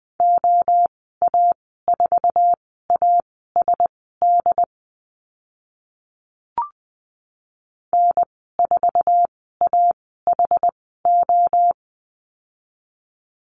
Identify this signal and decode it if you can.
Morse code, heard as a keyed tone, about 20 wpm, spaced as for 14 wpm, 700 Hz. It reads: OA4ASD N4AHO
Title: 069 Lesson 12g - Letter d - Call Signs 20wpm